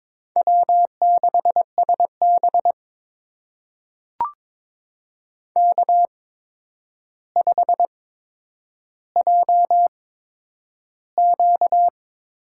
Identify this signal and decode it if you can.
Morse code, heard as a keyed tone, about 22 wpm, spaced as for 9 wpm, 700 Hz. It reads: W6SB K5JQ
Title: USA Call Signs - 4 Characters - 8x Character Spacing - 22wpm